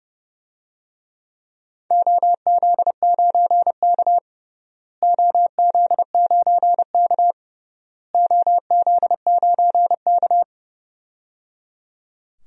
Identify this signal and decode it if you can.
Morse code, heard as a keyed tone, about 30 wpm, 700 Hz. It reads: OZ9K OZ9K OZ9K
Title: DXCC-100 Call Signs - 4 Characters - Repeated 3x - 30wpm